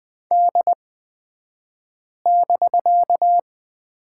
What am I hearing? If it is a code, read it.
Morse code, heard as a keyed tone, about 20 wpm, spaced as for 5 wpm, 700 Hz. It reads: D<BK>
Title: 254 Lesson 41d - Period - Two Characters 20wpm